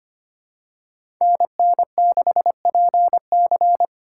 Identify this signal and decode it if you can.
Morse code, heard as a keyed tone, about 25 wpm, 700 Hz. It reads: NN6PC
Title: USA Call Signs - 5 Characters - 6x Character Spacing - 25wpm